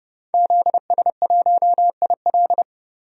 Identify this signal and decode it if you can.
Morse code, heard as a keyed tone, about 32 wpm, 700 Hz. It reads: ZS1IL
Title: DXCC-100 Call Signs - 5 Characters - 30wpm